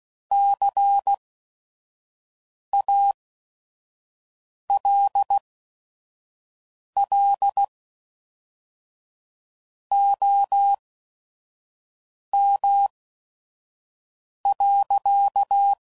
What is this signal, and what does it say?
Morse code, heard as a keyed tone, about 16 wpm, spaced as for 5 wpm, 800 Hz. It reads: CALL OM.
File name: CW Code Groups @5WPM / QSOs / kc7haa